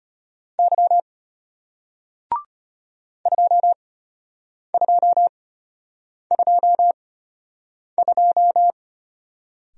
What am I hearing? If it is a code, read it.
Morse code, heard as a keyed tone, about 33 wpm, spaced as for 8 wpm, 700 Hz. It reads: Y 2222